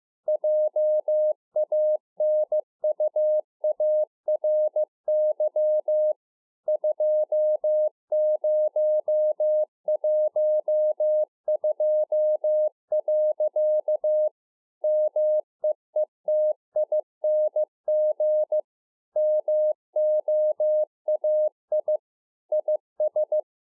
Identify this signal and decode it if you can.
Morse code, heard as a keyed tone, about 15 wpm, 610 Hz. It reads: JANUARY 2012. MEETING MOAI IS